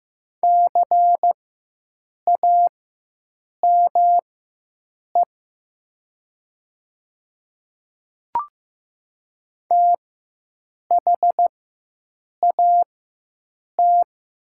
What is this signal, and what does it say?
Morse code, heard as a keyed tone, about 15 wpm, spaced as for 7 wpm, 700 Hz. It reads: CAME THAT
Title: 4x Character Spacing - Top 300 Words - 15wpm